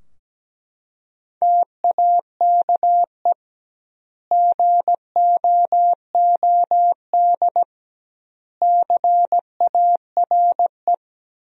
Morse code, heard as a keyed tone, about 17 wpm, 700 Hz. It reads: TAKE GOOD CARE
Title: 2x Word-Spacing - Sentences from Top 400 Words - 17wpm